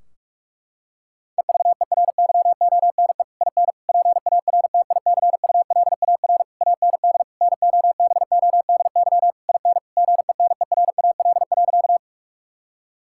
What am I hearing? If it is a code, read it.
Morse code, heard as a keyed tone, about 41 wpm, 700 Hz. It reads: EVERYONE IN PARTICULAR AND NOBODY IN GENERAL.
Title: Sentences from Top 1600 Words 45wpm